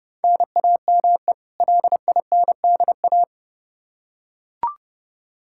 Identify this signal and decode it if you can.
Morse code, heard as a keyed tone, about 30 wpm, 700 Hz. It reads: NAME LINDA
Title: Practice QSO - Exchange 1 - Beginning - 30wpm